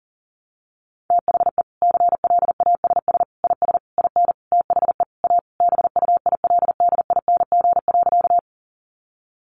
Morse code, heard as a keyed tone, about 40 wpm, 700 Hz. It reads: THE CLASS IS IN THE A BUILDING.
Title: Additional Easy Phrases 40wpm